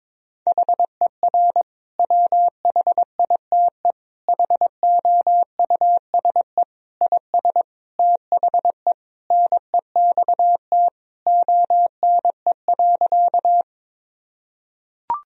Morse code, heard as a keyed tone, about 21 wpm, 700 Hz. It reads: HER WHITE HOUSE IS THE NEXT ONE.
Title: Sentences from Top 1000 Words - Review Part 1 - 22wpm